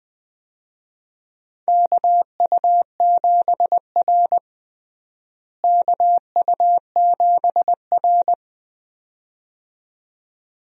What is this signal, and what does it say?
Morse code, heard as a keyed tone, about 21 wpm, 700 Hz. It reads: KU7R KU7R
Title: USA Call Signs - 4 Characters - Repeated 2x - 20wpm